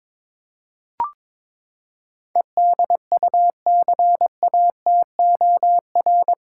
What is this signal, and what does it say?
Morse code, heard as a keyed tone, about 21 wpm, 700 Hz. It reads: EDUCATOR
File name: practice / 22wpm / Eight Letter Words - 22wpm